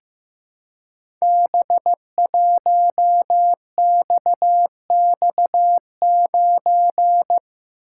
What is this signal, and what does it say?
Morse code, heard as a keyed tone, about 15 wpm, 700 Hz. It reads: B1XX9
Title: Code Groups - 5 Letter:Numbers - 15wpm